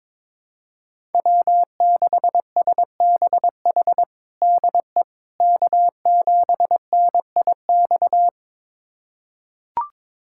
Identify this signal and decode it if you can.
Morse code, heard as a keyed tone, about 22 wpm, 700 Hz. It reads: W6SBH DE K7NIX
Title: USA Callsign DE Callsign 22wpm